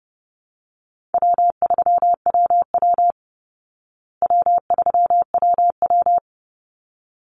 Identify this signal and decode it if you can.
Morse code, heard as a keyed tone, about 30 wpm, 700 Hz. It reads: W3WW W3WW
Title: Call Signs Repeated 3x - 30wpm